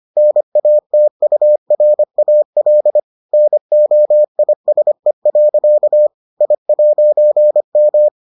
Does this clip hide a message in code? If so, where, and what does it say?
Yes: Morse code from 0.2 to 8.1 s, heard as a keyed tone, about 25 wpm, 600 Hz. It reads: NATURAL NOISE. I'M